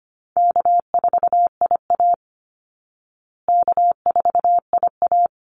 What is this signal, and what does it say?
Morse code, heard as a keyed tone, about 25 wpm, 700 Hz. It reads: K4IA K4IA
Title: Easy Call Signs Repeated 2x - 25wpm